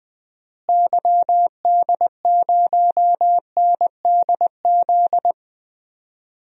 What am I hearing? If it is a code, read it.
Morse code, heard as a keyed tone, about 21 wpm, 700 Hz. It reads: YD0NDZ
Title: DXCC-100 Call Signs - 6 Characters - Repeated 3x - 20wpm